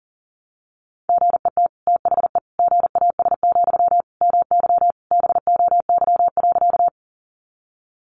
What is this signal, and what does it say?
Morse code, heard as a keyed tone, about 40 wpm, 700 Hz. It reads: GET THE GAS, MY BOY.
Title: Sentences from Top 1100 Words 40wpm